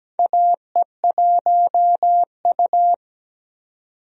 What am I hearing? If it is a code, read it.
Morse code, heard as a keyed tone, about 17 wpm, 700 Hz. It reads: AE1U
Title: USA Call Signs - 4 Characters - 17wpm